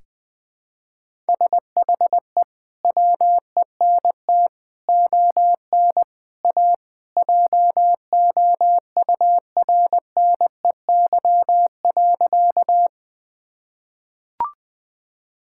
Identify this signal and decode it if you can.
Morse code, heard as a keyed tone, about 20 wpm, 700 Hz. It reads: SHE WENT ON A JOURNEY.
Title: Sentences with 2-Word Phrases - 20wpm